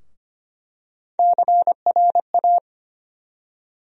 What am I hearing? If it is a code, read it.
Morse code, heard as a keyed tone, about 25 wpm, 700 Hz. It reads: CRA
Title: USA Call Sign Postfixes - 3 Letters 25wpm